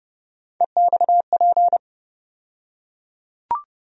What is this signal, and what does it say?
Morse code, heard as a keyed tone, about 30 wpm, 700 Hz. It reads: EXP
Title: USA Call Sign Postfixes - 3 Letters 30wpm